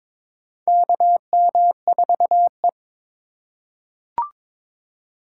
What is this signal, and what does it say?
Morse code, heard as a keyed tone, about 22 wpm, 700 Hz. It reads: KM4E
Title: USA Call Signs - 4 Characters - 4x Character Spacing - 22wpm